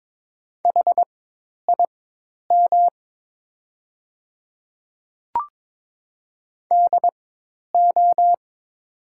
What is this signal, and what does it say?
Morse code, heard as a keyed tone, about 22 wpm, spaced as for 10 wpm, 700 Hz. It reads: HIM DO
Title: 4x Character Spacing - Top 100 Words - 22wpm